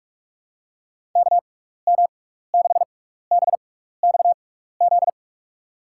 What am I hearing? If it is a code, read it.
Morse code, heard as a keyed tone, about 46 wpm, spaced as for 16 wpm, 700 Hz. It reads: KM6BXZ
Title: USA Call Signs - 6 Characters - 6x Character Spacing - 45wpm